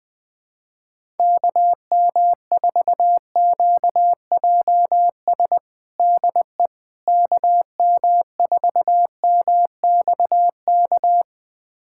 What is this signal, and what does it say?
Morse code, heard as a keyed tone, about 20 wpm, 700 Hz. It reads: KM4QJS DE KM4MXK
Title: USA Callsign DE Callsign - 6 Characters - 20wpm